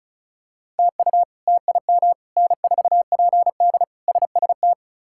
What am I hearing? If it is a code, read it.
Morse code, heard as a keyed tone, about 36 wpm, 700 Hz. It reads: TU TIM N4PD SST
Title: K1USN SST QSO Practice - 35wpm